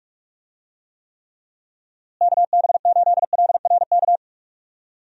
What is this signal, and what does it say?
Morse code, heard as a keyed tone, about 42 wpm, 700 Hz. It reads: KB8LRK
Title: USA Call Signs - 6 Characters - Repeated 3x - 45wpm